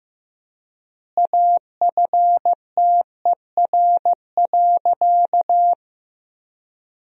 Morse code, heard as a keyed tone, about 16 wpm, 700 Hz. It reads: AFTER.
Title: 4x Word-Spacing - Sentences from Top 500 Words - 15wpm